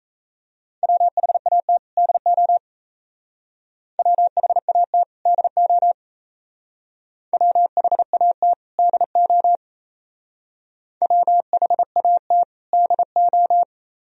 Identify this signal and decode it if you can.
Morse code, heard as a keyed tone, about 32 wpm, 700 Hz. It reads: WHAT DO WHAT DO WHAT DO WHAT DO
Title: Speed-Racing - Sets of 2 Words 28wpm